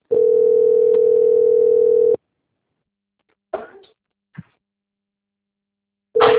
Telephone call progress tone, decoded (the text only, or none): ringback tone